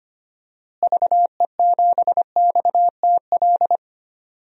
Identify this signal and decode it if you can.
Morse code, heard as a keyed tone, about 25 wpm, 700 Hz. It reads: VE7XTL
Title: DXCC-100 Call Signs - 6 Characters - Repeated 3x - 25wpm